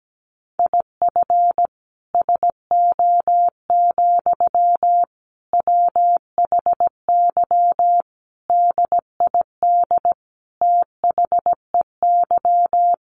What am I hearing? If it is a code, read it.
Morse code, heard as a keyed tone, about 17 wpm, 700 Hz. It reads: IF SO, WHY DID THEY
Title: Sentences from Top 1400 Words 17wpm